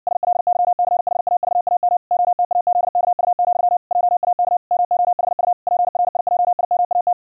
Morse code, heard as a keyed tone, about 60 wpm, 710 Hz. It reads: SLYCHAVAM OTAZKU, JAK MOHU PRIJIMAT
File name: hamradio / clanek-qrq-70wpm / 60wpm